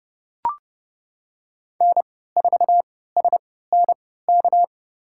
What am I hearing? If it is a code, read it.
Morse code, heard as a keyed tone, about 30 wpm, spaced as for 17 wpm, 700 Hz. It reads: N4SNK
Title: USA Call Signs - 3x Character Spacing - 30wpm